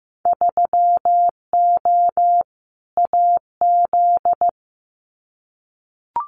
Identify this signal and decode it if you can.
Morse code, heard as a keyed tone, about 15 wpm, 700 Hz. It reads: 3O AZ
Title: Winter Field Day Exchanges 15wpm